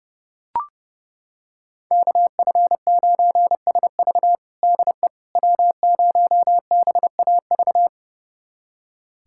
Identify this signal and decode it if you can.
Morse code, heard as a keyed tone, about 32 wpm, 700 Hz. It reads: KF9SV DE W0BAV
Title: USA Callsign DE Callsign - 5 Characters - 30wpm